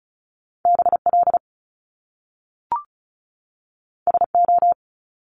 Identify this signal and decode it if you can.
Morse code, heard as a keyed tone, about 36 wpm, 700 Hz. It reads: BL SO
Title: Top Two Letter Combinations in Words 35wpm